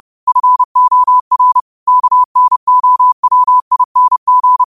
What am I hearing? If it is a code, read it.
Morse code, heard as a keyed tone, about 30 wpm, 1000 Hz. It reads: FOR KNOWING
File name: qotdfiles / 30wpm